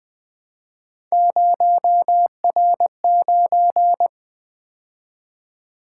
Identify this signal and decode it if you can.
Morse code, heard as a keyed tone, about 20 wpm, 700 Hz. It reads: 0R9
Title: Three Letter-Number Combinations - 20wpm